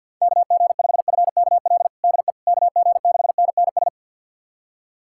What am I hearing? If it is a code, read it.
Morse code, heard as a keyed tone, about 52 wpm, 700 Hz. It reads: KG5FKL DE KG6NNS
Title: USA Callsign DE Callsign 50wpm